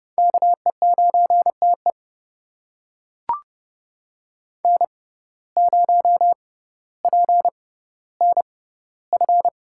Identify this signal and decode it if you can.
Morse code, heard as a keyed tone, about 30 wpm, spaced as for 15 wpm, 700 Hz. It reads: KE9TE N0PNF
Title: USA Call Signs - 5 Characters - 6x Character Spacing - 30wpm